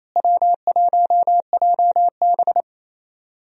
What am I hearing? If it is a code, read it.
Morse code, heard as a keyed tone, about 28 wpm, 700 Hz. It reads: W1JB
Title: USA Call Signs - 4 Characters - Repeated 2x - 28wpm